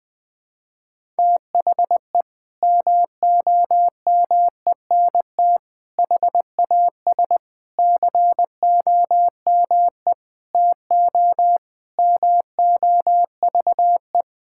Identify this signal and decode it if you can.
Morse code, heard as a keyed tone, about 20 wpm, 700 Hz. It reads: THE MOMENT HAS COME TO MOVE
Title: Sentences from Top 1000 Words - Review Part 5 - 20wpm